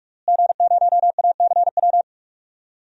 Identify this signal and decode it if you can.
Morse code, heard as a keyed tone, about 41 wpm, 700 Hz. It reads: G0ACW
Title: DXCC-100 Call Signs - 5 Characters - Repeated 3x - 45wpm